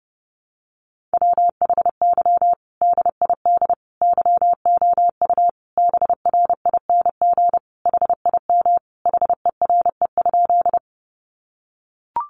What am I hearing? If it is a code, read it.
Morse code, heard as a keyed tone, about 30 wpm, 700 Hz. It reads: WHY DID YOU BRING HIM HERE?
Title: Sentences from Top 500 Words 30wpm